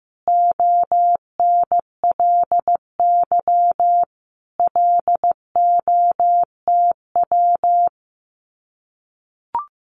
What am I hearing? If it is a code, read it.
Morse code, heard as a keyed tone, about 15 wpm, 700 Hz. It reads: ONLY LOTW
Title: Common QSO Phrases Encore 15wpm